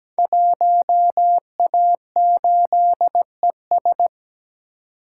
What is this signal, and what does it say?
Morse code, heard as a keyed tone, about 17 wpm, 700 Hz. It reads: 1A8ES
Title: Code Groups - 5 Letter:Numbers - 17wpm